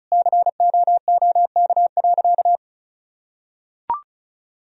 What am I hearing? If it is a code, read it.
Morse code, heard as a keyed tone, about 36 wpm, 700 Hz. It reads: COOK.
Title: Sentences from Top 1000 Words - Review Part 4 - 35wpm